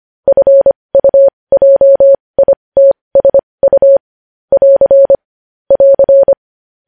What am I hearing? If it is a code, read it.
Morse code, heard as a keyed tone, about 25 wpm, spaced as for 20 wpm, 550 Hz. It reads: FUJITSU <AR> <AR>
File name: podcast / News headlines-1x-25x20